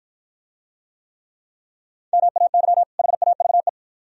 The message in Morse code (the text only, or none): MAY HAVE